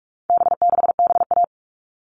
Morse code, heard as a keyed tone, about 44 wpm, 700 Hz. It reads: B6BA
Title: All Cut Numbers - 4-Digit 45wpm